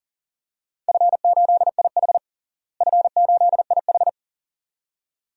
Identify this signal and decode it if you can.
Morse code, heard as a keyed tone, about 40 wpm, 700 Hz. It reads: F8IH F8IH
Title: DXCC-100 Call Signs - 4 Characters - Repeated 2x - 40wpm